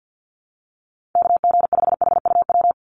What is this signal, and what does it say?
Morse code, heard as a keyed tone, about 50 wpm, 700 Hz. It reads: KG5HUW
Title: USA Call Signs 50wpm